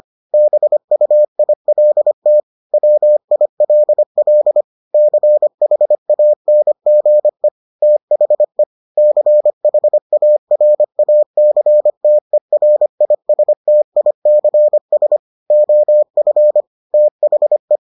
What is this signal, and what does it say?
Morse code, heard as a keyed tone, about 25 wpm, 600 Hz. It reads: BUILT WILL CHANGE THE CHARACTERISTICS OF THE